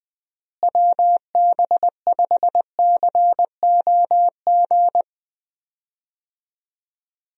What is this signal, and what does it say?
Morse code, heard as a keyed tone, about 19 wpm, 700 Hz. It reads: WB5COG
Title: USA Call Signs - 6 Characters - Repeated 3x - 20wpm